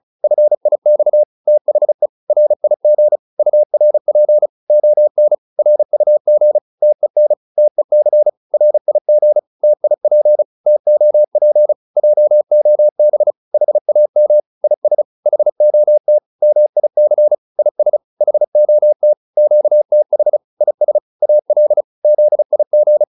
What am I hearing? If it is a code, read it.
Morse code, heard as a keyed tone, about 34 wpm, 600 Hz. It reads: FIX THE RIG URP ON RUG TEN TEC RIG TIP TOP JOB HAM IS HOT MIC IS HOT QTH IS AL ZIG